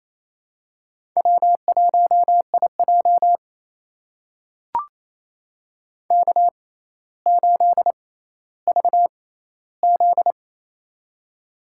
Morse code, heard as a keyed tone, about 29 wpm, spaced as for 14 wpm, 700 Hz. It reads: W1IJ K8VZ
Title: USA Call Signs - 4 Characters - 6x Character Spacing - 28wpm